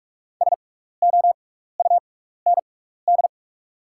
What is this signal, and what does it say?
Morse code, heard as a keyed tone, about 44 wpm, spaced as for 15 wpm, 700 Hz. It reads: SOUND
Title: 6x Character Spacing - Top 200 Words - 45wpm